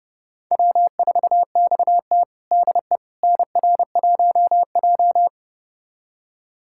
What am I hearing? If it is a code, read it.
Morse code, heard as a keyed tone, about 30 wpm, 700 Hz. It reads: W4XT DE NR1J
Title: USA Callsign DE Callsign - 4 Characters - 30wpm